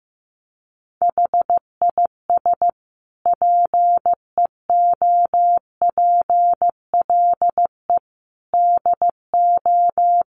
Morse code, heard as a keyed tone, about 15 wpm, 700 Hz. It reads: HIS PEOPLE DO
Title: Sentences from Top 1400 Words 15wpm